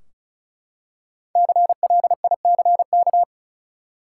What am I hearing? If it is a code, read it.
Morse code, heard as a keyed tone, about 34 wpm, 700 Hz. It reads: CLICK
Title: Five Letter Words - 35wpm